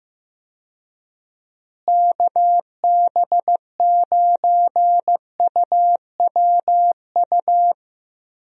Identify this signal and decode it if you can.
Morse code, heard as a keyed tone, about 15 wpm, 700 Hz. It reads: KB9UWU
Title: USA Call Signs - 6 Characters - Repeated 2x - 15wpm